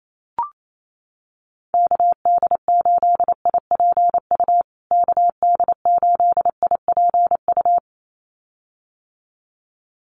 Morse code, heard as a keyed tone, about 28 wpm, 700 Hz. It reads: KD8IPU KD8IPU
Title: USA Call Signs Repeated 2x 28wpm